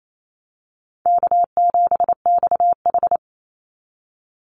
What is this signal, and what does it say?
Morse code, heard as a keyed tone, about 28 wpm, 700 Hz. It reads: K7XH
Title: Call Signs - 28wpm